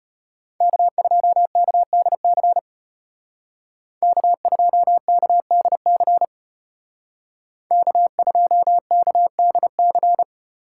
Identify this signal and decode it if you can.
Morse code, heard as a keyed tone, about 34 wpm, 700 Hz. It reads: K2KDC K2KDC K2KDC